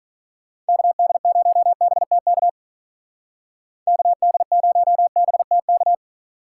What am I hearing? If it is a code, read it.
Morse code, heard as a keyed tone, about 45 wpm, 700 Hz. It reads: KD0BTK KD0BTK